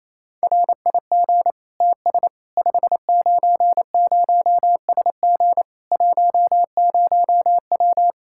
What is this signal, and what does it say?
Morse code, heard as a keyed tone, about 28 wpm, 700 Hz. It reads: RIG TS 590SG 10W
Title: Practice QSO - Exchange 2 - Info - 28wpm